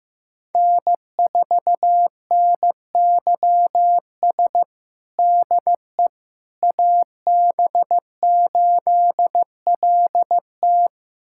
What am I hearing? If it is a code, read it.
Morse code, heard as a keyed tone, about 15 wpm, 700 Hz. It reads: N4NYS DE AB8LT
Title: USA Callsign DE Callsign - 5 Characters - 15wpm